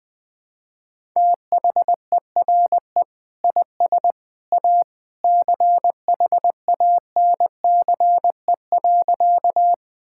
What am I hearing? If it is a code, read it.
Morse code, heard as a keyed tone, about 20 wpm, 700 Hz. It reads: THERE IS A CHANCE.
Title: Sentences with 2-Word Phrases - 20wpm